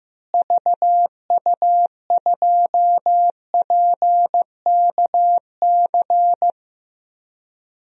Morse code, heard as a keyed tone, about 15 wpm, 700 Hz. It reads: VU2PKC
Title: DXCC-100 Call Signs - 6 Characters - Repeated 3x - 15wpm